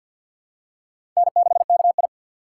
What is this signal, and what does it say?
Morse code, heard as a keyed tone, about 48 wpm, 700 Hz. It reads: N6KI
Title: USA Call Signs - 4 Characters - 3x Character Spacing - 50wpm